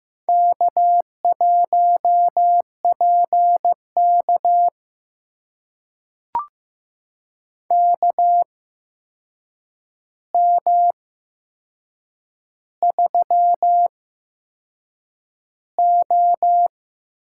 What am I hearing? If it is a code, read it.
Morse code, heard as a keyed tone, about 15 wpm, spaced as for 6 wpm, 700 Hz. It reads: K1PK KM3O
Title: USA Call Signs - 4 Characters - 8x Character Spacing - 15wpm